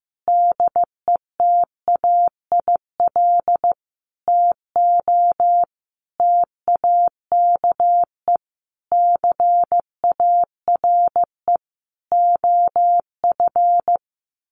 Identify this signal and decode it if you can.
Morse code, heard as a keyed tone, about 15 wpm, 700 Hz. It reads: DETAIL TO TAKE CARE OF
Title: Sentences from Top 1500 Words 15wpm